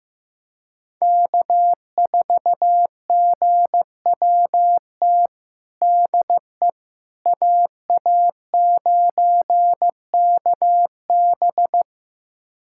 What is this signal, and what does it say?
Morse code, heard as a keyed tone, about 15 wpm, 700 Hz. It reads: K4GWT DE AA9KB
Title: USA Callsign DE Callsign 15wpm